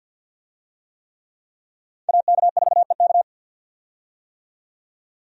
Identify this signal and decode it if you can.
Morse code, heard as a keyed tone, about 51 wpm, 700 Hz. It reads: AK4EK
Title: USA Call Signs - 5 Characters - 50wpm